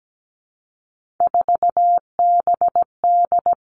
Morse code, heard as a keyed tone, about 17 wpm, 700 Hz. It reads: 4BD
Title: Three Challenging Letter-Number Combinations 17wpm